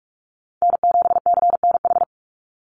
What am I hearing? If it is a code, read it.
Morse code, heard as a keyed tone, about 44 wpm, 700 Hz. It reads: N7CNH
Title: NA SOTA Chaser Callsigns 45wpm